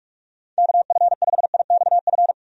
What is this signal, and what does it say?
Morse code, heard as a keyed tone, about 44 wpm, 700 Hz. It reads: KF5IXF
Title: USA Call Signs - 6 Characters - 45wpm